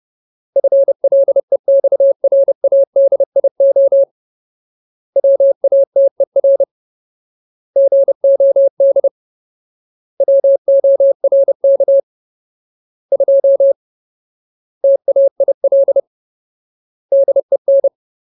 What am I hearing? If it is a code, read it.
Morse code, heard as a keyed tone, about 30 wpm, 550 Hz. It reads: FLEXRADIO WATER GOD WORK 2 TAIL DEN
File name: Ultimate Word List / 30WPM Ultimate